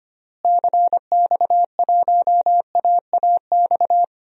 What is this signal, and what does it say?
Morse code, heard as a keyed tone, about 25 wpm, 700 Hz. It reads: CX1AAX
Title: DXCC-100 Call Signs - 6 Characters - Repeated 3x - 25wpm